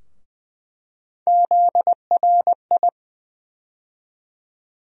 Morse code, heard as a keyed tone, about 20 wpm, 700 Hz. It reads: ZRI
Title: USA Call Sign Postfixes - 3 Letters 20wpm